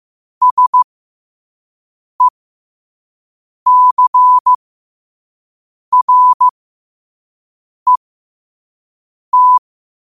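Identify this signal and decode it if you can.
Morse code, heard as a keyed tone, about 15 wpm, spaced as for 5 wpm, 1000 Hz. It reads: SECRET
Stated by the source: 5 wpm Morse.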